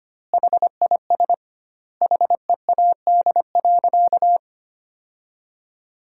Morse code, heard as a keyed tone, about 25 wpm, 700 Hz. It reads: HIS HEAD.
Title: 2x Word-Spacing - Sentences from Top 300 Words - 25wpm